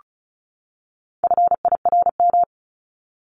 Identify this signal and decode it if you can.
Morse code, heard as a keyed tone, about 34 wpm, 700 Hz. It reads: FIRM
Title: Top 1251-1500 Words 35wpm